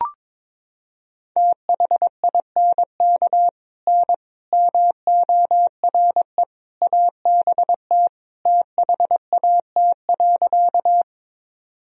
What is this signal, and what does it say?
Morse code, heard as a keyed tone, about 22 wpm, 700 Hz. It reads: THINK N MORE ABT THAT.
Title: Sentences from Top 200 Words and Abbreviations 22wpm